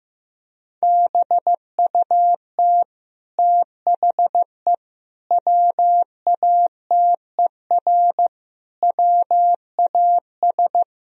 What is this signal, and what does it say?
Morse code, heard as a keyed tone, about 14 wpm, 700 Hz. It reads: BUT THE WATER WAS
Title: Sentences from Top 1000 Words - Review Part 3 - 15wpm